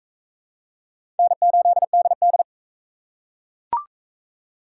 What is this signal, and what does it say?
Morse code, heard as a keyed tone, about 40 wpm, 700 Hz. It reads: N8DD